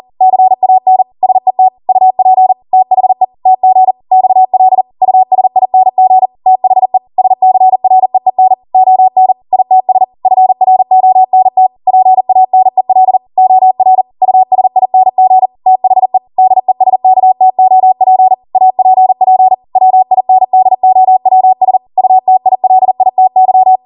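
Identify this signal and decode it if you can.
Morse code, heard as a keyed tone, about 40 wpm, 750 Hz. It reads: CAN SET UP THE TG XL USING THE SCREEN ON ITS FRONT PANEL OR USING THE DESKTOP APP WINDOWS UTILITY
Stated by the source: Code practice at 40 wpm.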